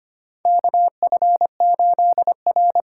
Morse code, heard as a keyed tone, about 25 wpm, 700 Hz. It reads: KF8R
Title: USA Call Signs - 6x Character Spacing - 25wpm